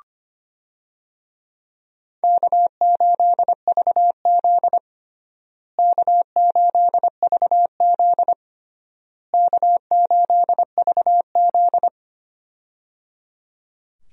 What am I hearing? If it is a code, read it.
Morse code, heard as a keyed tone, about 25 wpm, 700 Hz. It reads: K8VZ K8VZ K8VZ
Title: USA Call Signs - 4 Characters - Repeated 3x - 25wpm